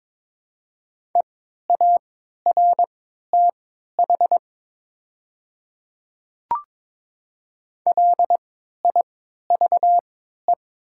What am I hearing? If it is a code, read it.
Morse code, heard as a keyed tone, about 22 wpm, spaced as for 12 wpm, 700 Hz. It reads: EARTH LIVE
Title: 3x Character Spacing - Top 200 Words - 22wpm